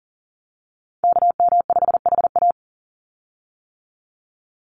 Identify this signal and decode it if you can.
Morse code, heard as a keyed tone, about 40 wpm, 700 Hz. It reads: KM5HA
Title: USA Call Signs 40wpm